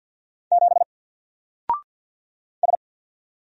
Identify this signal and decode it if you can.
Morse code, heard as a keyed tone, about 49 wpm, 700 Hz. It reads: 7 S
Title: Single Character (Letter:Number:Punctuation:Prosign) Alt 50wpm